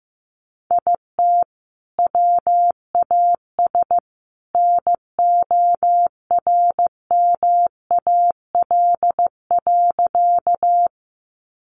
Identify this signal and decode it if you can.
Morse code, heard as a keyed tone, about 15 wpm, 700 Hz. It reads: IT WAS NORMAL.
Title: Sentences from Top 1200 Words 15wpm